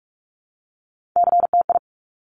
Morse code, heard as a keyed tone, about 46 wpm, 690 Hz. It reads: CTI
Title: Top Three Letter Combinations in Words 45wpm